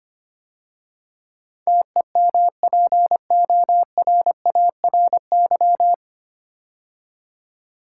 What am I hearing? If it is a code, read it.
Morse code, heard as a keyed tone, about 25 wpm, 700 Hz. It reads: TEMPORARY
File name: practice / 25wpm / 9 Letter Words - First 3 Letters Given - 25wpm